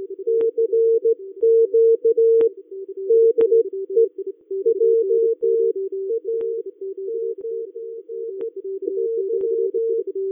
Morse code, heard as a keyed tone, about 15 wpm, spaced as for 12 wpm, 460 Hz. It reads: CQ DE JAMNY